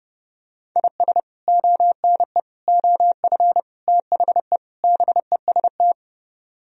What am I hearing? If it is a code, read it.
Morse code, heard as a keyed tone, about 28 wpm, 700 Hz. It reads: IS ONE OF THE BEST
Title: Speed-Racing - Sets of 5 Words 30wpm